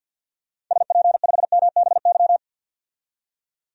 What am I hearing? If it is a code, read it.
Morse code, heard as a keyed tone, about 50 wpm, 700 Hz. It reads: SP5MBY
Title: DXCC-100 Call Signs - 6 Characters - 50wpm